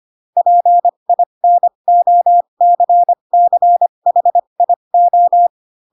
Morse code, heard as a keyed tone, about 25 wpm, spaced as for 22 wpm, 700 Hz. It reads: PINOCCHIO
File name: Pinocchio / Pinocchio ch6 18 25 ik1hge